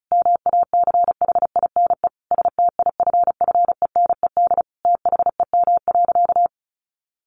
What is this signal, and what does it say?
Morse code, heard as a keyed tone, about 34 wpm, 700 Hz. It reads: MACHINE STIFFENED THEM.
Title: A Sound Of Thunder 35wpm